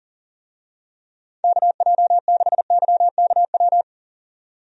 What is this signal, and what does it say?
Morse code, heard as a keyed tone, about 38 wpm, 700 Hz. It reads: KJ6YKW
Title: USA Call Signs - 6 Characters - 40wpm